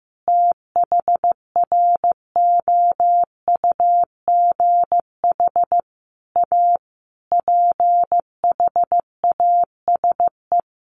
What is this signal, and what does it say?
Morse code, heard as a keyed tone, about 15 wpm, 700 Hz. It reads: THROUGH A PHASE
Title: Sentences from Top 1200 Words 15wpm